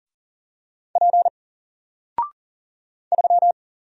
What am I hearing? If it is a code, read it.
Morse code, heard as a keyed tone, about 39 wpm, 700 Hz. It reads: P 3
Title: Single Letter-Number 40wpm